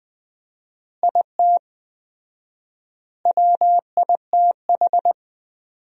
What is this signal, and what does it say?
Morse code, heard as a keyed tone, about 20 wpm, 700 Hz. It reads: IT WITH
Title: 4x Word-Spacing - Sentences from Top 500 Words - 20wpm